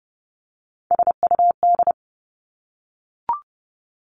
Morse code, heard as a keyed tone, about 30 wpm, 700 Hz. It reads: SUD
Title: Three Challenging Letter-Number Combinations 30wpm